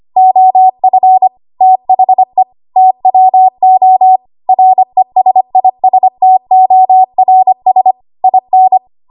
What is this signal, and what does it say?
Morse code, heard as a keyed tone, about 25 wpm, 750 Hz. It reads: OF THE TWO RESISTORS IN
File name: Archive / 25 WPM / 25WPM